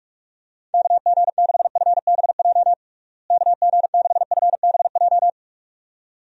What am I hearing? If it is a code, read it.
Morse code, heard as a keyed tone, about 41 wpm, 700 Hz. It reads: KG6FBJ KG6FBJ
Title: USA Call Signs - 6 Characters - Repeated 2x - 45wpm